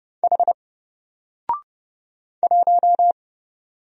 Morse code, heard as a keyed tone, about 30 wpm, 700 Hz. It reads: H 1
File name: Single-Letter-Number-30wpm